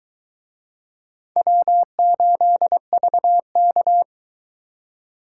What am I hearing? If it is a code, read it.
Morse code, heard as a keyed tone, about 22 wpm, 700 Hz. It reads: W8VK